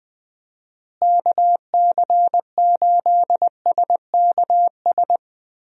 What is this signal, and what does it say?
Morse code, heard as a keyed tone, about 20 wpm, 700 Hz. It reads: KC8SKS
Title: USA Call Signs - 6 Characters - Repeated 3x - 20wpm